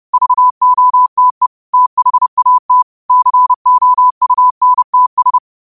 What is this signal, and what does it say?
Morse code, heard as a keyed tone, about 30 wpm, 1000 Hz. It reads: VOTE THAT COUNTS